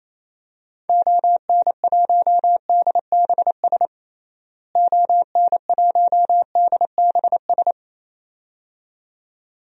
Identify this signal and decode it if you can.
Morse code, heard as a keyed tone, about 28 wpm, 700 Hz. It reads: ON1DBS ON1DBS
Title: DXCC-100 Call Signs - 6 Characters - Repeated 2x - 28wpm